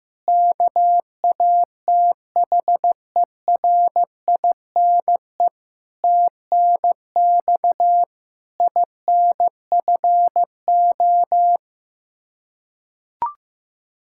Morse code, heard as a keyed tone, about 15 wpm, 700 Hz. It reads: KATHERINE TNX INFO
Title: K1USN SST QSO Practice 15wpm